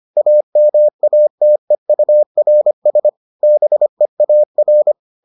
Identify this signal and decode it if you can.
Morse code, heard as a keyed tone, about 25 wpm, 600 Hz. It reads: AMATEURS BEAR